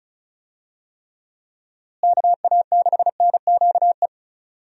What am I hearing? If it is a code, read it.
Morse code, heard as a keyed tone, about 36 wpm, 700 Hz. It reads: KA6NQE
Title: USA Call Signs - 6 Characters - Repeated 3x - 35wpm